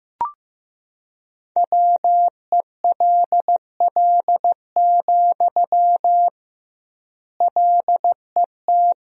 Morse code, heard as a keyed tone, about 15 wpm, 700 Hz. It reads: WELL, LET
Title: 2x Word-Spacing - Sentences from Top 300 Words - 15wpm